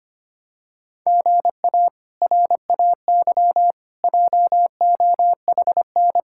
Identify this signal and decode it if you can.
Morse code, heard as a keyed tone, about 25 wpm, 700 Hz. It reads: GA RAY JOHN
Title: K1USN SST QSO Practice - 25wpm